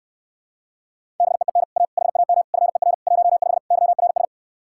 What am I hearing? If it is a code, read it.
Morse code, heard as a keyed tone, about 69 wpm, 700 Hz. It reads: BEEN A HAM FER 15 YRS